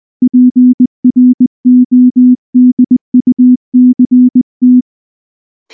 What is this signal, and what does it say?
Morse code, heard as a keyed tone, about 19 wpm, 250 Hz. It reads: PRODUCT
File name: morse / morse-training-20WPM-10